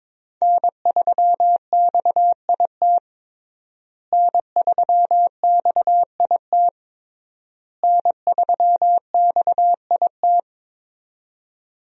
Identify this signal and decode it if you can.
Morse code, heard as a keyed tone, about 22 wpm, 700 Hz. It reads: N3XIT N3XIT N3XIT
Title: USA Call Signs - 5 Characters - Repeated 3x - 22wpm